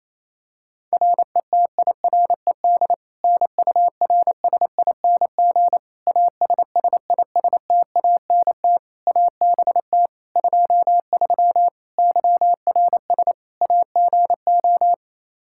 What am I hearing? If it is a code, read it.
Morse code, heard as a keyed tone, about 28 wpm, 700 Hz. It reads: RETIRED NURSING ASSISTANT ABT 23 YRS AGO
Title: Practice QSO - Exchange 3 - More Info - 28wpm